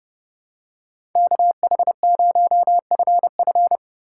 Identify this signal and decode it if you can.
Morse code, heard as a keyed tone, about 31 wpm, 700 Hz. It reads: KH0FF